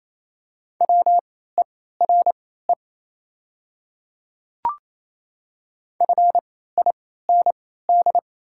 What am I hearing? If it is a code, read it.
Morse code, heard as a keyed tone, about 28 wpm, spaced as for 16 wpm, 690 Hz. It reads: WERE FIND
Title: 3x Character Spacing - Top 100 Words - 28wpm